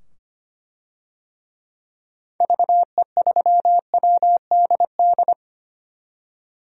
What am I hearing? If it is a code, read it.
Morse code, heard as a keyed tone, about 25 wpm, 700 Hz. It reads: VE3WDD
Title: DXCC-100 Call Signs - 6 Characters - 25wpm